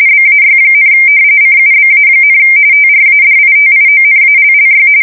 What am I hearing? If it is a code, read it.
RTTY, about 45.45 baud. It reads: BAY, CUBA, TRADITIONALLY
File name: fsk170 / battleshipnj